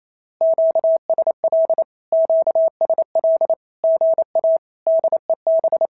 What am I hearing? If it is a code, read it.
Morse code, heard as a keyed tone, about 28 wpm, 650 Hz. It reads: QSL QSL GA DEB
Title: SOTA QSO Practice 28wpm